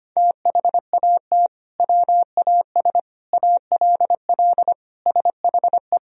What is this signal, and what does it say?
Morse code, heard as a keyed tone, about 25 wpm, 700 Hz. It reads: THAT WAS ALL SHE
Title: Sentences from Top 100 Words and Abbreviations 25wpm